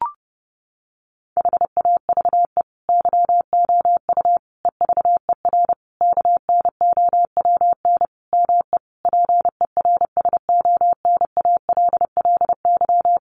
Morse code, heard as a keyed tone, about 30 wpm, 700 Hz. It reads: HAVE YOU EVER KNOWN ME PERSONALLY
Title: Sentences from Top 1400 Words 30wpm